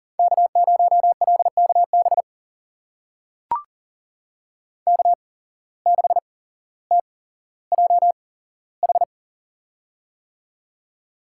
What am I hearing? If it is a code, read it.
Morse code, heard as a keyed tone, about 40 wpm, spaced as for 17 wpm, 700 Hz. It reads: K0LKB K6TJH
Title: USA Call Signs - 8x Character Spacing - 40wpm